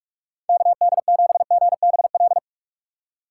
Morse code, heard as a keyed tone, about 46 wpm, 700 Hz. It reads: KD7GBL
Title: USA Call Signs - 6 Characters - 3x Character Spacing - 45wpm